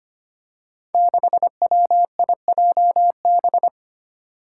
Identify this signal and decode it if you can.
Morse code, heard as a keyed tone, about 25 wpm, 700 Hz. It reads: 6WIJB
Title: Five Challenging Letter-Number Combinations 25wpm